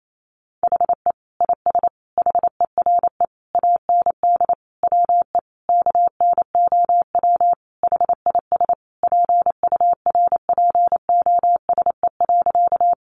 Morse code, heard as a keyed tone, about 28 wpm, 700 Hz. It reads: HE IS HERE AND WE KNOW HIS PURPOSE.
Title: Sentences from Top 800 Words 28wpm